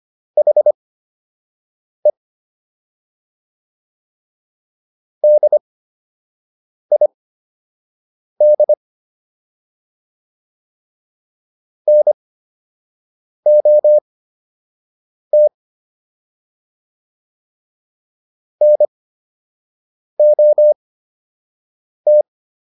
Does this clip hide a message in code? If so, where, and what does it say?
Yes: Morse code from 0.4 to 22.2 s, heard as a keyed tone, about 25 wpm, spaced as for 6 wpm, 600 Hz. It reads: HE DID NOT NOT